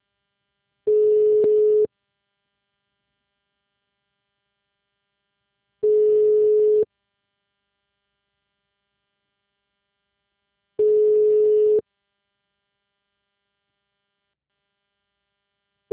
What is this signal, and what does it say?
Ringback tone (Europe, 425 Hz).
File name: out- -2020- - - .12